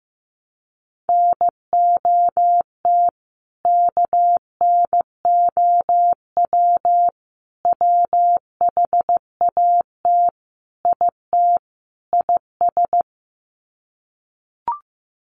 Morse code, heard as a keyed tone, about 15 wpm, 700 Hz. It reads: NOT KNOW WHAT IT IS
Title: Sets of 5 Words Encore 15wpm